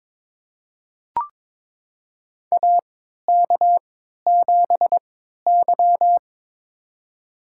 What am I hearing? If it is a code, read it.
Morse code, heard as a keyed tone, about 22 wpm, spaced as for 13 wpm, 700 Hz. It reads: AK7Y
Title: USA Call Signs - 3x Character Spacing - 22wpm